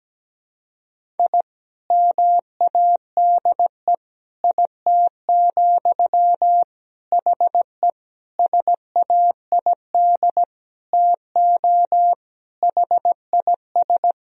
Morse code, heard as a keyed tone, about 17 wpm, 700 Hz. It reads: I MADE IT, HE SAID TO HIS
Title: Sentences from Top 1000 Words - Review Part 1 - 17wpm